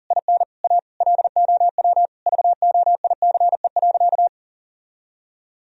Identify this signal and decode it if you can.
Morse code, heard as a keyed tone, about 40 wpm, 700 Hz. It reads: IN A LOW VOICE.
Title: Sentences from Top 1000 Words - Review Part 1 - 40wpm